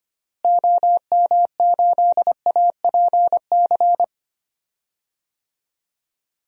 Morse code, heard as a keyed tone, about 25 wpm, 700 Hz. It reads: OM8APC
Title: DXCC-100 Call Signs - 6 Characters - Repeated 3x - 25wpm